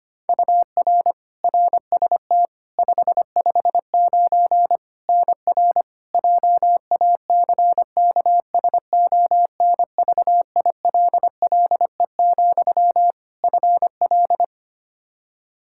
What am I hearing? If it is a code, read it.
Morse code, heard as a keyed tone, about 25 wpm, 700 Hz. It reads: UR RST 559 NR JACKSONVILLE, FL
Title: Practice QSO - Exchange 1 - Beginning - 25wpm